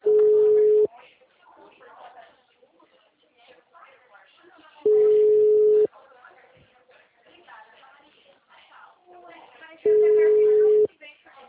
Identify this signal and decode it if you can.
Ringback tone (Europe, 425 Hz), with speech.